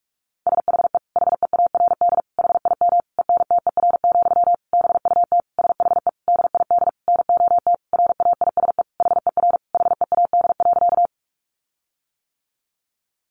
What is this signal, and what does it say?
Morse code, heard as a keyed tone, about 44 wpm, 700 Hz. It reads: SHE HEARD HIM ENTER, BUT SHE DID NOT RAISE HER HEAD.
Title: Sentences from Top 1400 Words 45wpm